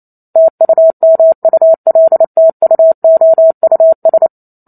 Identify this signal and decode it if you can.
Morse code, heard as a keyed tone, about 28 wpm, 650 Hz. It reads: TUMULTUOUS